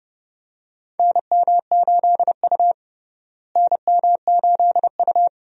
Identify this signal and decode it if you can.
Morse code, heard as a keyed tone, about 29 wpm, 700 Hz. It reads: NM8U NM8U
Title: USA Call Signs - 4 Characters - Repeated 2x - 30wpm